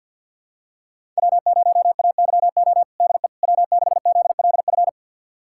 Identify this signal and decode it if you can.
Morse code, heard as a keyed tone, about 52 wpm, 700 Hz. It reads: W0AYO DE W6ZLF
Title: USA Callsign DE Callsign - 5 Characters - 50wpm